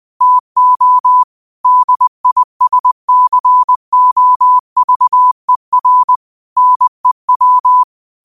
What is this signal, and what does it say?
Morse code, heard as a keyed tone, about 20 wpm, 1000 Hz. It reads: TO DISCOVER NEW
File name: qotdfiles / 20wpm